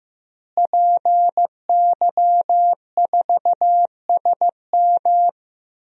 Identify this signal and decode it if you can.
Morse code, heard as a keyed tone, about 15 wpm, 700 Hz. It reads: PY4SM
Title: DXCC-100 Call Signs - 5 Characters - Repeated 3x - 15wpm